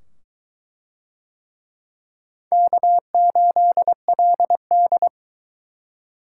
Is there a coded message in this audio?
Yes: Morse code from 2.5 to 5.1 s, heard as a keyed tone, about 20 wpm, 700 Hz. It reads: K8LD